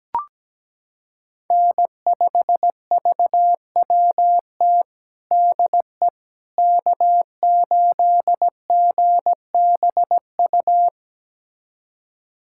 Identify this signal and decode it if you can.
Morse code, heard as a keyed tone, about 17 wpm, 700 Hz. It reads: N5VWT DE K8GBU
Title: USA Callsign DE Callsign - 5 Characters - 17wpm